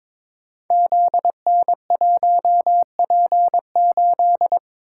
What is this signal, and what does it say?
Morse code, heard as a keyed tone, about 22 wpm, 700 Hz. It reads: ZN1P8
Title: Code Groups - 5 Letter:Numbers - 22wpm